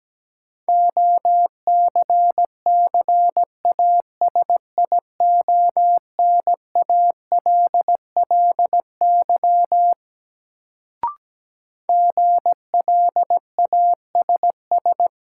Morse code, heard as a keyed tone, about 17 wpm, 700 Hz. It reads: OCCASIONALLY GLASS
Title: Sentences from Top 1300 Words - New Word Drill Alt - 17wpm